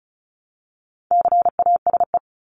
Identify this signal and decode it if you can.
Morse code, heard as a keyed tone, about 34 wpm, 700 Hz. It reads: CASE
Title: ES:S Suffix - Top 100 Words with Base Words 35wpm